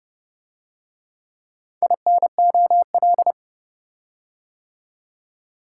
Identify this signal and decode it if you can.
Morse code, heard as a keyed tone, about 29 wpm, 700 Hz. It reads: INOL